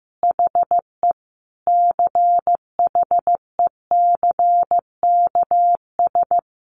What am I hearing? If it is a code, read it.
Morse code, heard as a keyed tone, about 15 wpm, 700 Hz. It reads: HE CHECKS
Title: Easy Phrases 15wpm